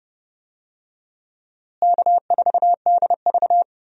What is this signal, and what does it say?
Morse code, heard as a keyed tone, about 30 wpm, 700 Hz. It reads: K4DV
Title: USA Call Signs - 4 Characters - Repeated 2x - 30wpm